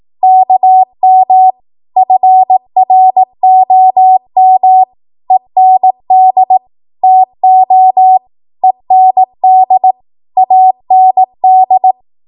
Morse code, heard as a keyed tone, about 18 wpm, 750 Hz. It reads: KM FROM END TO END AND
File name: Morse / 18WPM